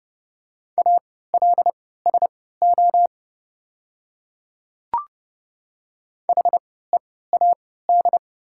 Morse code, heard as a keyed tone, about 30 wpm, spaced as for 17 wpm, 690 Hz. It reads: ALSO HEAD
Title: 3x Character Spacing - Top 300 Words - 30wpm